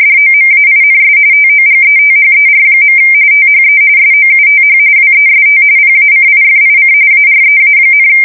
RTTY, about 45.45 baud. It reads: DOMAINS TOGETHER. THE SOUTH WEST PACIFIC AREA,
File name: fsk170 / usnavy4